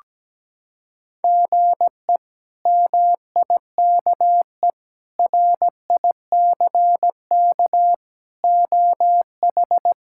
Morse code, heard as a keyed tone, about 17 wpm, 700 Hz. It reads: GE MIKE RICK OH
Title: K1USN SST QSO Practice - 17wpm